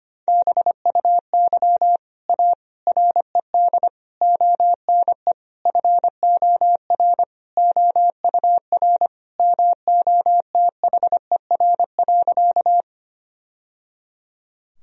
Morse code, heard as a keyed tone, about 25 wpm, 700 Hz. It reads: BUY A RED ONE FOR OUR MOTHER.
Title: Sentences from Top 1000 Words - Review Part 2 - 25wpm